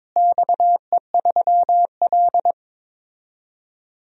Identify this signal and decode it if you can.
Morse code, heard as a keyed tone, about 22 wpm, 700 Hz. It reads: XE3L
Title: DXCC-100 Call Signs - 4 Characters - Repeated 3x - 22wpm